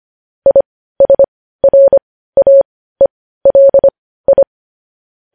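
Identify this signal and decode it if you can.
Morse code, heard as a keyed tone, about 25 wpm, spaced as for 15 wpm, 550 Hz. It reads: ISRAELI
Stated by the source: Morse practice sending news headlines at 15 wpm.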